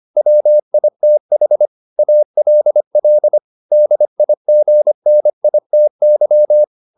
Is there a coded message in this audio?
Yes: Morse code from 0.2 to 6.6 s, heard as a keyed tone, about 25 wpm, 600 Hz. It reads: WITH ALL DIGNITY